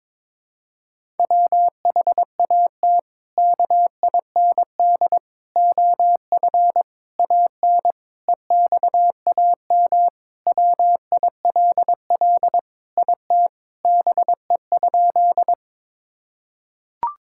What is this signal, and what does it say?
Morse code, heard as a keyed tone, about 22 wpm, 700 Hz. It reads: WHAT KIND OF AN EXAM WILL IT BE?
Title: Sentences from Top 1000 Words - Review Part 5 - 22wpm